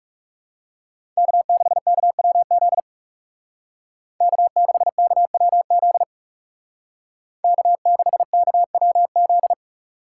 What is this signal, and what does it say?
Morse code, heard as a keyed tone, about 37 wpm, 700 Hz. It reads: K6KWZ K6KWZ K6KWZ
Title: Speed Racing - USA 1x3 Call Signs 30wpm